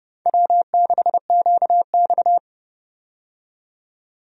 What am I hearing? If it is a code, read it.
Morse code, heard as a keyed tone, about 31 wpm, 700 Hz. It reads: W6QX
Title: USA Call Signs - 4 Characters - 30wpm